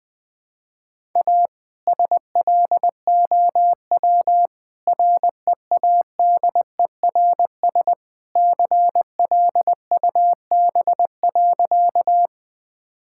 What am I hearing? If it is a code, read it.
Morse code, heard as a keyed tone, about 20 wpm, 700 Hz. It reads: A SLOW READERS CLUB.
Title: All-Age Appropriate Jokes - 20wpm